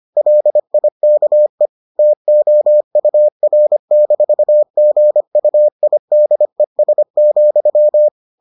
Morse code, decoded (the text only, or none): LIKE TOUR-GUIDES,